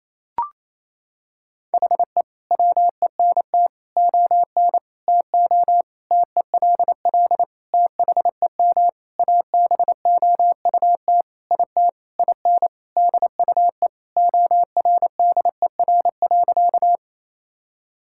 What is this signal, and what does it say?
Morse code, heard as a keyed tone, about 28 wpm, 700 Hz. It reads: HE WENT ON TO TELL THEM ABOUT IT IN DUE ORDER.
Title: Sentences from Top 1000 Words - Review Part 6 - 28wpm